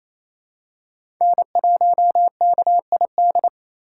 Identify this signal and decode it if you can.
Morse code, heard as a keyed tone, about 28 wpm, 700 Hz. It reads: N1KID
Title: USA Call Signs - 2x Character Spacing - 28wpm